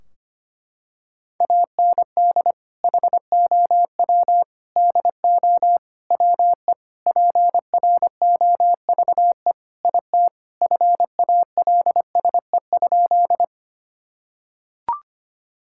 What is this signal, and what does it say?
Morse code, heard as a keyed tone, about 25 wpm, 700 Hz. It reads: AND HOW DO WE PROVE IT FALSE?
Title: Sentences from Top 1600 Words 25wpm